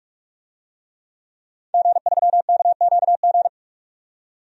Morse code, heard as a keyed tone, about 46 wpm, 700 Hz. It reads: G3KQG
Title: DXCC-100 Call Signs - 5 Characters - 45wpm